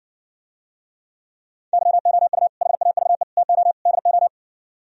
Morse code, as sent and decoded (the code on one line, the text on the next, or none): -.-- --- ..- / .... .- ...- . / - --- / -.. ---
YOU HAVE TO DO